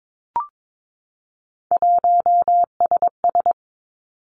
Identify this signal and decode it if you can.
Morse code, heard as a keyed tone, about 22 wpm, 700 Hz. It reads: 1SS
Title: Three Challenging Letter-Number Combinations 22wpm